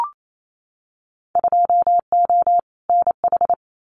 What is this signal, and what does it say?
Morse code, heard as a keyed tone, about 28 wpm, 700 Hz. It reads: 2O NH
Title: Winter Field Day Exchanges 28wpm